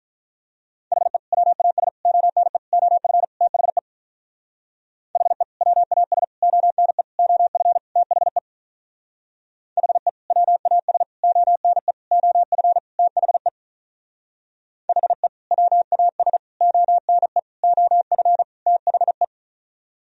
Morse code, heard as a keyed tone, about 43 wpm, 700 Hz. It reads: HE WAS ONE OF THE HE WAS ONE OF THE HE WAS ONE OF THE HE WAS ONE OF THE
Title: Speed-Racing - Sets of 5 Words 35wpm